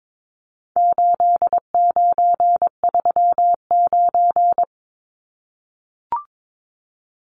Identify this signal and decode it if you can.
Morse code, heard as a keyed tone, about 22 wpm, 700 Hz. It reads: 8939
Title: Numbers - 4-Digit 22wpm